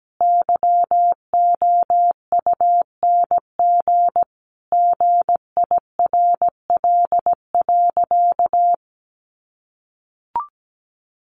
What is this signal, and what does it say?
Morse code, heard as a keyed tone, about 17 wpm, 700 Hz. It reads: YOUNG GIRL.
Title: Sentences from Top 1400 Words 17wpm